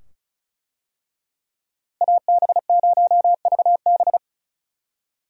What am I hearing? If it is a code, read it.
Morse code, heard as a keyed tone, about 36 wpm, 700 Hz. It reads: AB0VB
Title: USA Call Signs - 5 Characters - Repeated 2x - 35wpm